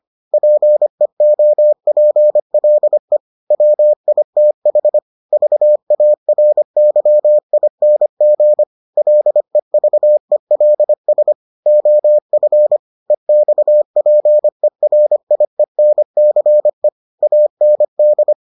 Morse code, heard as a keyed tone, about 25 wpm, 600 Hz. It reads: PEOPLE WITH VARYING LEVELS OF EXPERIENCE AND